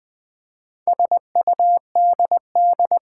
Morse code, heard as a keyed tone, about 20 wpm, 700 Hz. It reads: SUDD
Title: Build A Word - 2 Letters At A Time - 8-Letter Words - 20wpm